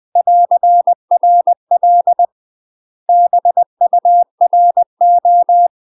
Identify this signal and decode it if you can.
Morse code, heard as a keyed tone, about 20 wpm, 700 Hz. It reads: <AR>RL BURO